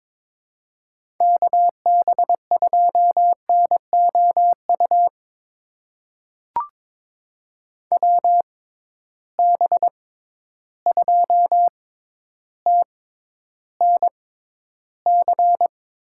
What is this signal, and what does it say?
Morse code, heard as a keyed tone, about 22 wpm, spaced as for 11 wpm, 700 Hz. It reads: KB2NOU WB2TNC
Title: USA Call Signs - 6 Characters - 6x Character Spacing - 22wpm